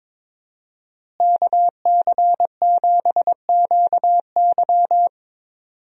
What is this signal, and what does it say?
Morse code, heard as a keyed tone, about 22 wpm, 700 Hz. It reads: KC7QY
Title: USA Call Signs - 5 Characters - Repeated 2x - 22wpm